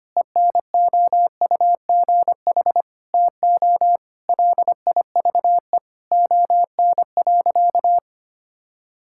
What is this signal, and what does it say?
Morse code, heard as a keyed tone, about 25 wpm, 700 Hz. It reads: ENOUGH TO LIVE ON.
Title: Sentences from Top 1000 Words - Review Part 6 - 25wpm